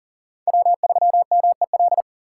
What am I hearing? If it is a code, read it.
Morse code, heard as a keyed tone, about 37 wpm, 700 Hz. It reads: W3MEL
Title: K1USN SST QSO Practice - 40wpm